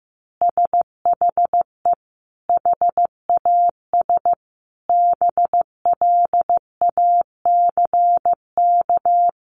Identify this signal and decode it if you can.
Morse code, heard as a keyed tone, about 15 wpm, 700 Hz. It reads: SHE HAS BLACK
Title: Easy Phrases 15wpm